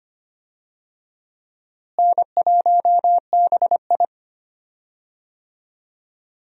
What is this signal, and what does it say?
Morse code, heard as a keyed tone, about 25 wpm, 700 Hz. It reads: N1BI
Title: USA Call Signs - 4 Characters - Repeated 2x - 25wpm